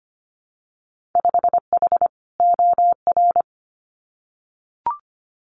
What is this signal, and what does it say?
Morse code, heard as a keyed tone, about 25 wpm, 700 Hz. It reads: 5H OR
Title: Winter Field Day Exchanges Encore 25wpm